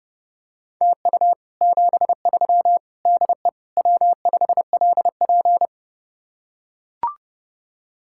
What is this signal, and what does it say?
Morse code, heard as a keyed tone, about 30 wpm, 700 Hz. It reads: TU 73 DE W5LP